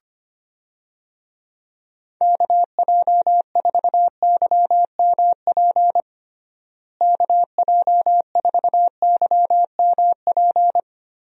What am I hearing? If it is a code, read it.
Morse code, heard as a keyed tone, about 25 wpm, 700 Hz. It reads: KJ4YMP KJ4YMP
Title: USA Call Signs - 6 Characters - Repeated 3x - 25wpm